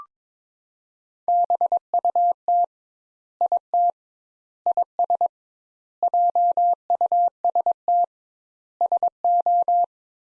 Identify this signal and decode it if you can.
Morse code, heard as a keyed tone, about 22 wpm, 700 Hz. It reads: BUT IT IS JUST SO
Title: Sentences Top 200 Words - 22wpm